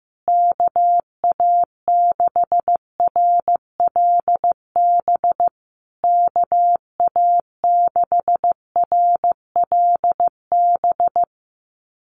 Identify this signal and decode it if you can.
Morse code, heard as a keyed tone, about 15 wpm, 700 Hz. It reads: KA6RLB KA6RLB
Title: USA Call Signs Repeated 3x 15wpm